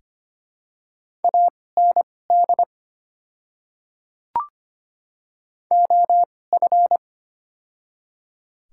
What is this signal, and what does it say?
Morse code, heard as a keyed tone, about 25 wpm, spaced as for 18 wpm, 700 Hz. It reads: AND OF
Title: 2x Character Spacing - Top 100 Words - 25wpm